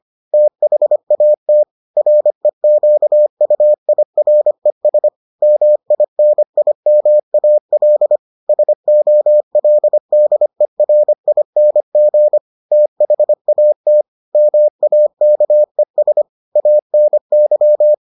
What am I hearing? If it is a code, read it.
Morse code, heard as a keyed tone, about 25 wpm, 600 Hz. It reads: THAT REQUIRES MINIMAL SOLDERING THAT MAKES ANY